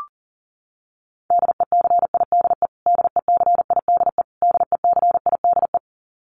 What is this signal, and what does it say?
Morse code, heard as a keyed tone, about 40 wpm, 700 Hz. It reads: DECIDE DECIDE DECIDE
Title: Top 500 Words Repeated 40wpm